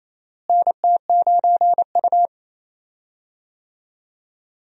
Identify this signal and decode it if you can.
Morse code, heard as a keyed tone, about 28 wpm, 700 Hz. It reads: NT9U
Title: USA Call Signs - 4 Characters - Repeated 3x - 28wpm